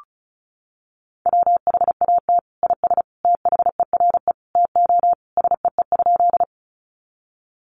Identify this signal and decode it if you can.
Morse code, heard as a keyed tone, about 34 wpm, 700 Hz. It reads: WHAT IS THERE TO SEE?
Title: Sentences from Top 100 Words 35wpm